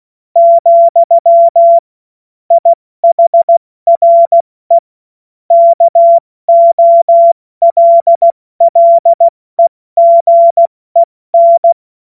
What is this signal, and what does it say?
Morse code, heard as a keyed tone, about 16 wpm, 680 Hz. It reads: , IHRE KOLLEGEN
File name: Morsekurs / Witze / Witz 24.Oktober 2023 70 F80